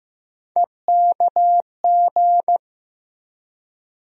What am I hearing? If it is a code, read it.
Morse code, heard as a keyed tone, about 15 wpm, 700 Hz. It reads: EKG
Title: USA Call Sign Postfixes - 3 Letters 15wpm